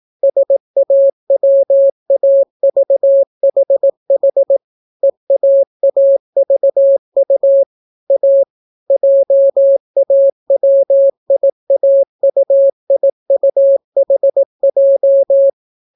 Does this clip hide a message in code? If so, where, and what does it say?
Yes: Morse code from 0.2 to 15.5 s, heard as a keyed tone, about 18 wpm, 550 Hz. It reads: SAWAVHH EAAVU A JAWIAUIUHJ